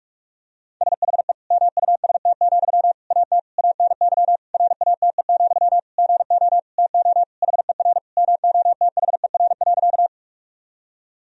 Morse code, heard as a keyed tone, about 46 wpm, 700 Hz. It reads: SHE MUST, AT ANY RATE, GO TO HER MOTHER.
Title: Sentences from Top 1000 Words - Review Part 4 - 45wpm